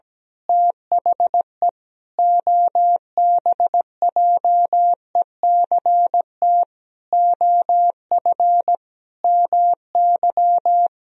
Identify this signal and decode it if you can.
Morse code, heard as a keyed tone, about 17 wpm, 700 Hz. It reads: THE OBJECT OF MY
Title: Sentences from Top 1000 Words - Review Part 3 - 17wpm